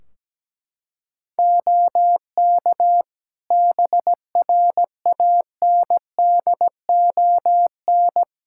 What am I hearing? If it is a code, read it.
Morse code, heard as a keyed tone, about 17 wpm, 700 Hz. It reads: OK BRANDON
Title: K1USN SST QSO Practice 17wpm